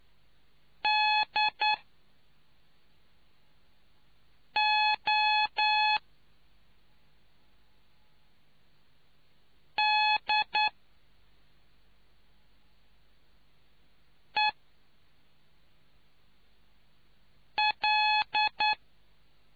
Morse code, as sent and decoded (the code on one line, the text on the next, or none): -.. --- / -.. .-..
DO DEL